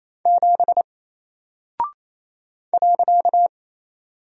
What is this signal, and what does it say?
Morse code, heard as a keyed tone, about 27 wpm, 700 Hz. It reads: 7 .
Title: Single Character (Letter:Number:Punctuation:Prosign) Alt 28wpm